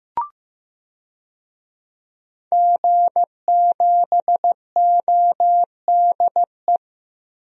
Morse code, heard as a keyed tone, about 15 wpm, 700 Hz. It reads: G7ODE
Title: DXCC-100 Call Signs - 5 Characters - Repeated 2x - 15wpm